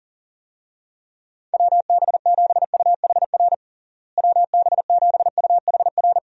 Morse code, heard as a keyed tone, about 37 wpm, 700 Hz. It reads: WB7UHR WB7UHR
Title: USA Call Signs - 6 Characters - Repeated 3x - 40wpm